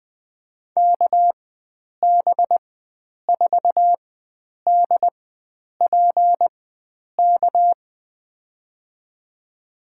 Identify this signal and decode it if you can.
Morse code, heard as a keyed tone, about 20 wpm, spaced as for 9 wpm, 700 Hz. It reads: KB4DPK
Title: USA Call Signs - 4x Character Spacing - 20wpm